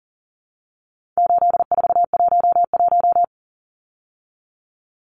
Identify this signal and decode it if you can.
Morse code, heard as a keyed tone, about 40 wpm, 700 Hz. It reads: 8411
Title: Numbers - 4-Digit 40wpm